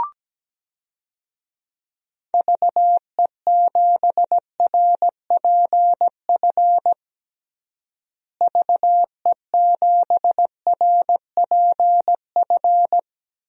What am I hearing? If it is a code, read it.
Morse code, heard as a keyed tone, about 17 wpm, 700 Hz. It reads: VE7RPF VE7RPF
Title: DXCC-100 Call Signs - 6 Characters - Repeated 2x - 17wpm